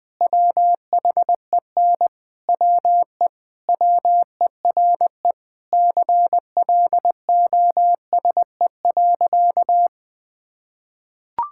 Morse code, heard as a keyed tone, about 20 wpm, 700 Hz. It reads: WHEN WE WERE CLOSE.
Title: Sentences with 5-Word Phrases - Spoken Beginning - 20wpm